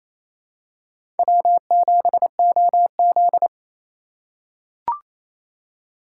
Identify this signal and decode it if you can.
Morse code, heard as a keyed tone, about 28 wpm, 700 Hz. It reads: W7OZ
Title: USA Call Signs - 2x Character Spacing - 28wpm